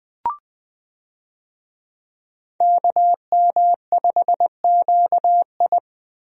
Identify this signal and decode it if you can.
Morse code, heard as a keyed tone, about 20 wpm, 700 Hz. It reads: KM5QI
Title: USA Call Signs - 5 Characters - Repeated 3x - 20wpm